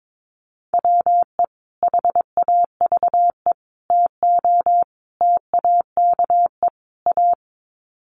Morse code, heard as a keyed tone, about 22 wpm, 700 Hz. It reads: WE HAVE TO TAKE A
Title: Sets of 5 Words Encore 22wpm